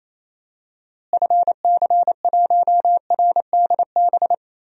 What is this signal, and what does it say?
Morse code, heard as a keyed tone, about 28 wpm, 700 Hz. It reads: FC1RDB
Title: DXCC-100 Call Signs - 6 Characters - Repeated 3x - 28wpm